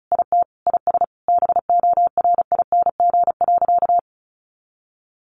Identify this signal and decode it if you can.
Morse code, heard as a keyed tone, about 34 wpm, 700 Hz. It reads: IT IS BORING.
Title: Easy Phrases 35wpm